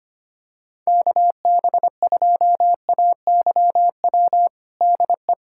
Morse code, heard as a keyed tone, about 25 wpm, 700 Hz. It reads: KB2AYW DE
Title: USA Callsign DE Callsign - 6 Characters - 25wpm